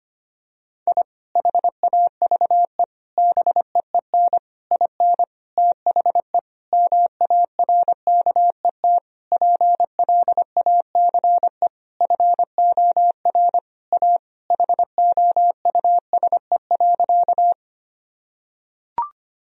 Morse code, heard as a keyed tone, about 25 wpm, 700 Hz. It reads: I HAVE BEEN IN THE MARKET PLACE FOR A HOUSE.
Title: Sentences from Top 1000 Words - Review Part 3 - 25wpm